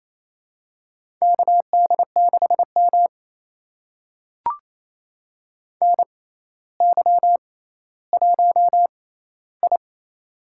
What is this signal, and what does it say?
Morse code, heard as a keyed tone, about 28 wpm, spaced as for 14 wpm, 700 Hz. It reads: KD6M NY1I
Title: USA Call Signs - 4 Characters - 6x Character Spacing - 28wpm